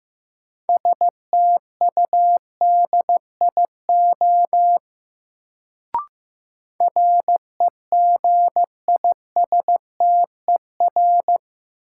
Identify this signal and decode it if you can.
Morse code, heard as a keyed tone, about 16 wpm, 700 Hz. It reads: STUDIO REGISTER
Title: Sentences from Top 1200 Words - New Word Drill Alt - 15wpm